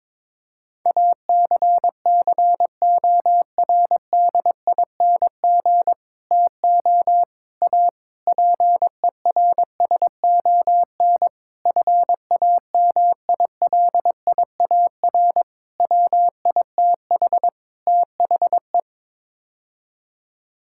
Morse code, heard as a keyed tone, about 21 wpm, 700 Hz. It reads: ACCORDING TO A PERSON FAMILIAR WITH THE
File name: practice / 22wpm / Sets of 7 Words - 22wpm